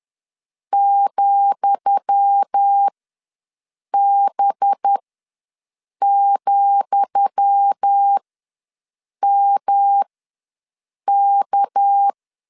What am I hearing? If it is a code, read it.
Morse code, heard as a keyed tone, about 10 wpm, spaced as for 6 wpm, 800 Hz. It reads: ,B,MK
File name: LEZIONI / 10 SIGNS / 10D COMA ws 10wpm 4314